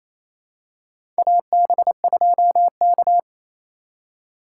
Morse code, heard as a keyed tone, about 28 wpm, 700 Hz. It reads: AB2K
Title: USA Call Signs - 4 Characters - 3x Character Spacing - 28wpm